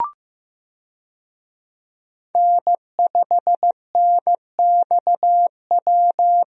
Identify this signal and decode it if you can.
Morse code, heard as a keyed tone, about 15 wpm, 700 Hz. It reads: N5NXW
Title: USA Call Signs - 5 Characters - Repeated 2x - 15wpm